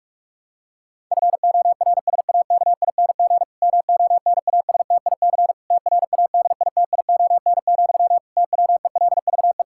Morse code, heard as a keyed tone, about 46 wpm, 700 Hz. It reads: FORSAKING MONASTIC TRADITION, TWELVE
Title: Pangram Challenge v2 - 45wpm